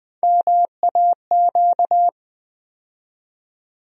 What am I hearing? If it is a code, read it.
Morse code, heard as a keyed tone, about 20 wpm, 700 Hz. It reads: MAQ
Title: USA Call Sign Postfixes - 3 Letters 20wpm